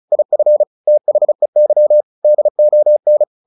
Morse code, heard as a keyed tone, about 34 wpm, 600 Hz. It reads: IF THEY DON
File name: the old man and the sea / the old man and the sea 1 35wpm